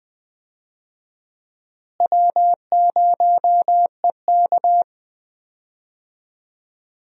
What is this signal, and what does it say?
Morse code, heard as a keyed tone, about 21 wpm, 700 Hz. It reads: W0EK
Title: USA Call Signs - 4 Characters - Repeated 3x - 20wpm